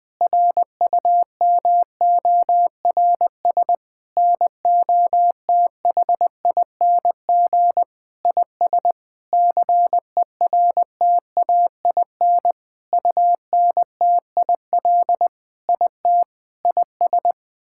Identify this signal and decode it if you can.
Morse code, heard as a keyed tone, about 21 wpm, 700 Hz. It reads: RUMORS NOTHING IS CERTAIN UNTIL IT IS
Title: Sentences with 8-Word Phrases - 20wpm